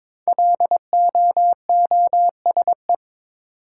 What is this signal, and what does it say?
Morse code, heard as a keyed tone, about 22 wpm, 700 Hz. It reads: LOOSE
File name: 5 Letter Words - First 3 Letters Given - 22wpm